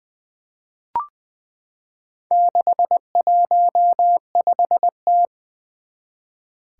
Morse code, heard as a keyed tone, about 20 wpm, 700 Hz. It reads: 615T
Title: Four Challenging Letter-Number Combinations 20wpm